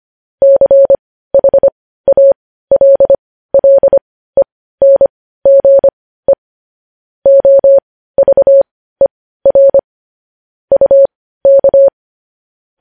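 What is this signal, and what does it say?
Morse code, heard as a keyed tone, about 25 wpm, spaced as for 15 wpm, 550 Hz. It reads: CHALLENGE OVER UK
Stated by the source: Morse news headlines practice, possibly sent at 15 wpm.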